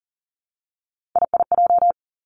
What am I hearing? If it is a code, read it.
Morse code, heard as a keyed tone, about 40 wpm, 700 Hz. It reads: IIJ
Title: Three Challenging Letter-Number Combinations 40wpm